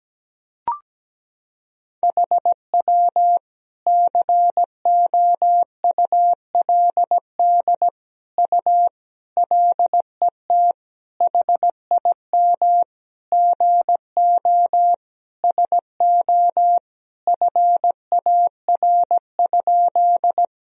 Morse code, heard as a keyed tone, about 17 wpm, 700 Hz. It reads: HW COULD U LET HIM GO SO FAR?
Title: Sentences from Top 300 Words and Abbreviations 17wpm